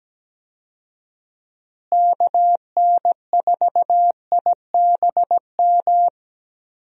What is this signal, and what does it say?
Morse code, heard as a keyed tone, about 17 wpm, 700 Hz. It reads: KN4IBM
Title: USA Call Signs - 6 Characters - Repeated 2x - 17wpm